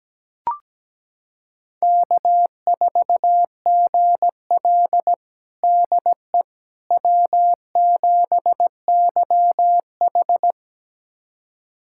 Morse code, heard as a keyed tone, about 17 wpm, 700 Hz. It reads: K4GL DE W7YH
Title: USA Callsign DE Callsign - 4 Characters - 17wpm